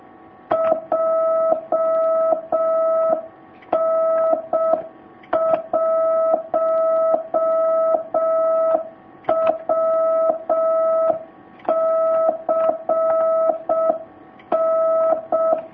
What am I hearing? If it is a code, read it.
Morse code, heard as a keyed tone, about 6 wpm, 650 Hz. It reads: JN1WCN
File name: jh8cbh / 2019cw